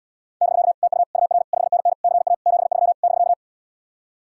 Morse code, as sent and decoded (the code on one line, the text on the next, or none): -...-.- / - ..- / ..- .-. / ..... -. -. / -.-. .- / --... ...-- / -...-.-
<BK> TU UR 5NN CA 73 <BK>